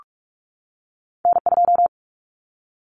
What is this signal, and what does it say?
Morse code, heard as a keyed tone, about 46 wpm, 700 Hz. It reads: N2
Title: Common Cut Numbers - 2-Digit 45wpm